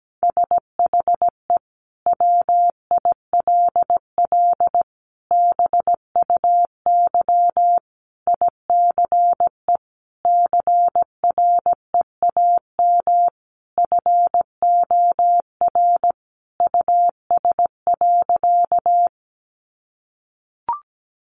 Morse code, heard as a keyed tone, about 17 wpm, 700 Hz. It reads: SHE WILL BUY ICE CREAM FOR US.
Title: Sentences from Top 1500 Words 17wpm